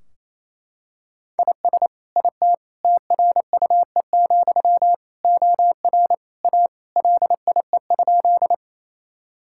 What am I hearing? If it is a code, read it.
Morse code, heard as a keyed tone, about 28 wpm, 700 Hz. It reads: IS IT TRUE, OR A LIE?
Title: Sentences from Top 1600 Words 28wpm